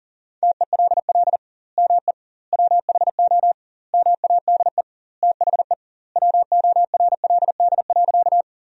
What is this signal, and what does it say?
Morse code, heard as a keyed tone, about 37 wpm, 700 Hz. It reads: TELL ME WHO MADE THE WORLD.
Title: 2x Word-Spacing - Sentences from Top 200 Words - 40wpm